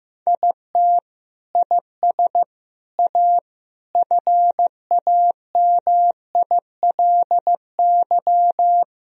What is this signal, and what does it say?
Morse code, heard as a keyed tone, about 15 wpm, 700 Hz. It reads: IT IS A FAMILY
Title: Sentences from Top 1000 Words - Review Part 2 - 15wpm